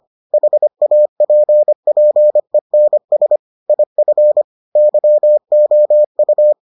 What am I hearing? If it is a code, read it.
Morse code, heard as a keyed tone, about 25 wpm, 600 Hz. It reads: HAPPENS IF YOU